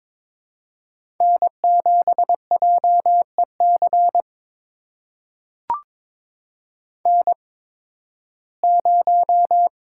Morse code, heard as a keyed tone, about 22 wpm, spaced as for 14 wpm, 700 Hz. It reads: N7JEC N0
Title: USA Call Signs - 5 Characters - 8x Character Spacing - 22wpm